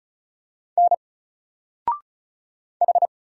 Morse code, heard as a keyed tone, about 36 wpm, 700 Hz. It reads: N H
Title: Single Character (Letter:Number:Punctuation:Prosign) Alt 35wpm